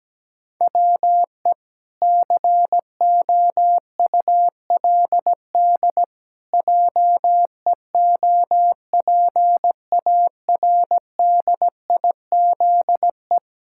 Morse code, heard as a keyed tone, about 17 wpm, 700 Hz. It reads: WE COULD JEOPARDIZE
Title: Pangram Challenge v2 - 17wpm